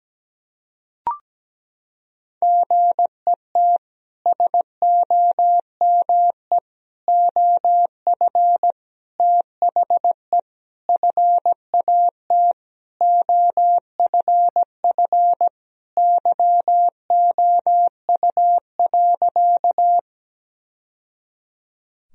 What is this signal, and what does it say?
Morse code, heard as a keyed tone, about 17 wpm, 700 Hz. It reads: GET SOME OF THE FAT OFF YOU.
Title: Sentences from Top 1000 Words - Review Part 4 - 17wpm